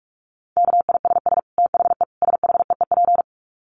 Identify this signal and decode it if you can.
Morse code, heard as a keyed tone, about 44 wpm, 700 Hz. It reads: KISS THE SHEEP
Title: Dit Challenge 45wpm